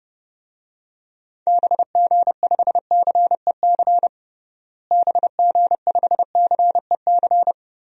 Morse code, heard as a keyed tone, about 32 wpm, 700 Hz. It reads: BG5CEC BG5CEC
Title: DXCC-100 Call Signs - 6 Characters - Repeated 3x - 30wpm